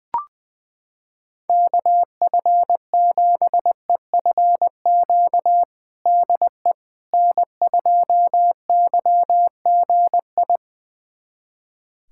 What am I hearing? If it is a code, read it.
Morse code, heard as a keyed tone, about 21 wpm, 700 Hz. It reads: KF7EFQ DE N2YGI
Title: USA Callsign DE Callsign 20wpm